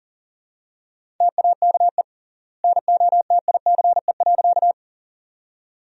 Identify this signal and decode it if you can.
Morse code, heard as a keyed tone, about 40 wpm, 700 Hz. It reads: TAKE NOTICE.
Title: 3x Word-Spacing - Sentences from Top 400 Words - 40wpm